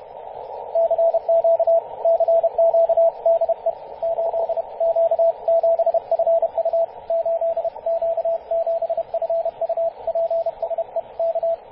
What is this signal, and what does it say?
Morse code, heard as a keyed tone, about 31 wpm, 670 Hz, with radio noise. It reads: CQ CQ DE 8Q7FU 8Q7FU PSE K
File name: 8q7fu-24-896mhz-cw